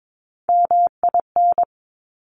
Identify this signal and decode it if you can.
Morse code, heard as a keyed tone, about 22 wpm, 700 Hz. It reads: MIN
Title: Top Three Letter Combinations in Words 22wpm